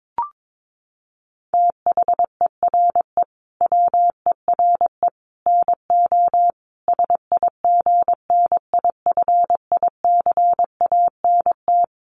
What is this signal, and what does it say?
Morse code, heard as a keyed tone, about 22 wpm, 700 Hz. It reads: THERE WERE NO SIGNIFICANT
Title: Sets of 5 Words Encore2 22wpm